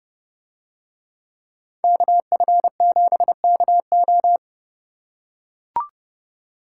Morse code, heard as a keyed tone, about 30 wpm, 700 Hz. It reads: KF7KO